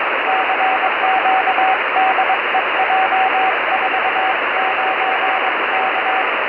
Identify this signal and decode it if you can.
Morse code, heard as a keyed tone, about 21 wpm, 760 Hz, with radio noise. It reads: CQ DE J49K
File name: 70mhz / j49k pe1itr 70200khz